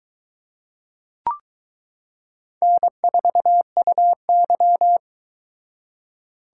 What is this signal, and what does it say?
Morse code, heard as a keyed tone, about 22 wpm, 700 Hz. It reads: N4UY